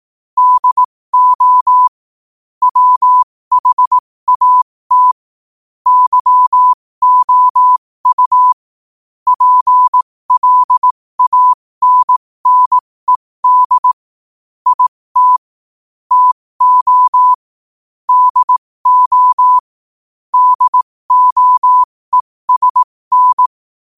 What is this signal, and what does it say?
Morse code, heard as a keyed tone, about 18 wpm, spaced as for 15 wpm, 1000 Hz. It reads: DO WHAT YOU PLANNED IT TO DO DOESN
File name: qotdfiles / 15wpm